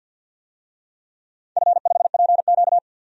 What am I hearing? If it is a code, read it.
Morse code, heard as a keyed tone, about 48 wpm, 700 Hz. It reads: F5PQ
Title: DXCC-100 Call Signs - 4 Characters - 50wpm